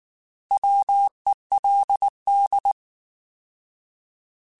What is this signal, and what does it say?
Morse code, heard as a keyed tone, about 19 wpm, 800 Hz. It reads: WELD
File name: Alphabetical / Alphabetical mp3s / Ws @18